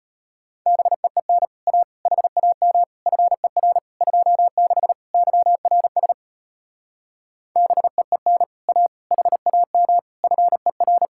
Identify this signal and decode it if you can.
Morse code, heard as a keyed tone, about 33 wpm, 700 Hz. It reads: BEEN A HAM FER 26 YRS BEEN A HAM FER